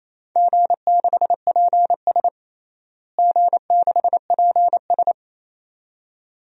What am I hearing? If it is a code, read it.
Morse code, heard as a keyed tone, about 28 wpm, 700 Hz. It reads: G6PS G6PS
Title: DXCC-100 Call Signs - 4 Characters - Repeated 2x - 28wpm